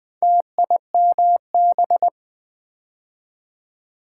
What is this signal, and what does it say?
Morse code, heard as a keyed tone, about 20 wpm, 700 Hz. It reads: TIMB
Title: Build A Word - 2 Letters At A Time - 6-Letter Words - 20wpm